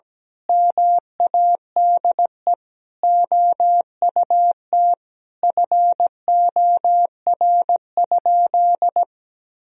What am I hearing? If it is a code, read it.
Morse code, heard as a keyed tone, about 17 wpm, 700 Hz. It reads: MADE OUT FOR?
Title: Sentences from Top 1000 Words - Review Part 3 - 17wpm